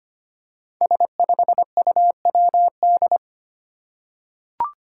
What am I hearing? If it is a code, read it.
Morse code, heard as a keyed tone, about 25 wpm, 700 Hz. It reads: S5UWD
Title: Five Challenging Letter-Number Combinations 25wpm